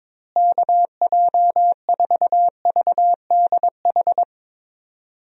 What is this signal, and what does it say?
Morse code, heard as a keyed tone, about 22 wpm, 700 Hz. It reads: KJ4VDH
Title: USA Call Signs - 6 Characters - Repeated 2x - 22wpm